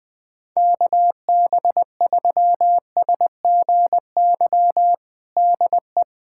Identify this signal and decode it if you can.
Morse code, heard as a keyed tone, about 20 wpm, 700 Hz. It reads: KB3SGY DE
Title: USA Callsign DE Callsign - 6 Characters - 20wpm